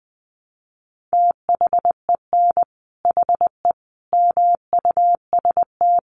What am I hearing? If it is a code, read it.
Morse code, heard as a keyed tone, about 20 wpm, 700 Hz. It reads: THEN HE MUST
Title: Sentences from Top 500 Words 20wpm